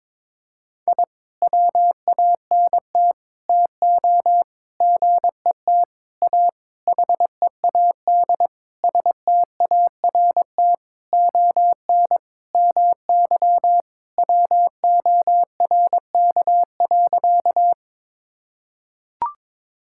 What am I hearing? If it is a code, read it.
Morse code, heard as a keyed tone, about 22 wpm, 700 Hz. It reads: I WANT TO GET A HEAD START ON MY WORK.
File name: Sentences with 7-Word Phrases - 22wpm